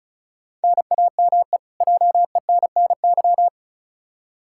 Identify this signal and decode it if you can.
Morse code, heard as a keyed tone, about 36 wpm, 700 Hz. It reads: NAME JENNY
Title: K1USN SST QSO Practice 35wpm